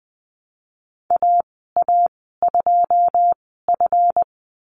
Morse code, heard as a keyed tone, about 20 wpm, spaced as for 15 wpm, 700 Hz. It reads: AA2F
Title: 140 Lesson 23g - Letter f - Call Signs 20wpm